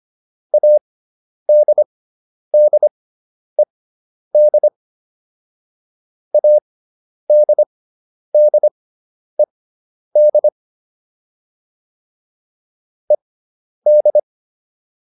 Morse code, heard as a keyed tone, about 25 wpm, spaced as for 10 wpm, 600 Hz. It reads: ADDED ADDED ED
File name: ED5 10